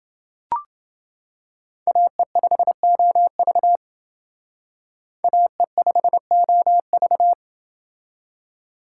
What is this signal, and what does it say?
Morse code, heard as a keyed tone, about 30 wpm, 700 Hz. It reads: AE5OV AE5OV